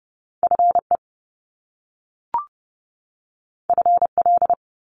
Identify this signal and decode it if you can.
Morse code, heard as a keyed tone, about 30 wpm, 700 Hz. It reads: FE FL
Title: Top Two Letter Combinations in Words 30wpm